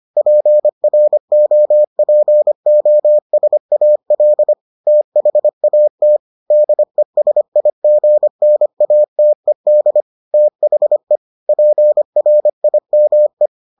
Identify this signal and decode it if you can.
Morse code, heard as a keyed tone, about 25 wpm, 600 Hz. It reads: PROPOSAL THAT DESIGNATED THE PRIME